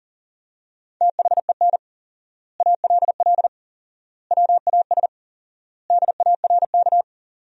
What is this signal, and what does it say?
Morse code, heard as a keyed tone, about 40 wpm, 700 Hz. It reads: THEN ALL WAS DARK
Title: 4x Word-Spacing - Sentences from Top 500 Words - 40wpm